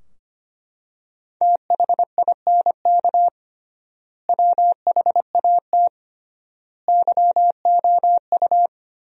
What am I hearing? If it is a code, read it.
Morse code, heard as a keyed tone, about 25 wpm, 700 Hz. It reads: THINK WHAT YOU
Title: 3x Word-Spacing - Sentences from Top 200 Words - 25wpm